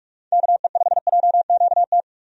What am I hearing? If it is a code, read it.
Morse code, heard as a keyed tone, about 46 wpm, 700 Hz. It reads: KE5JQT
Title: USA Call Signs - 6 Characters - 45wpm